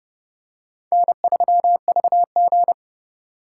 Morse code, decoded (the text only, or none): N3VG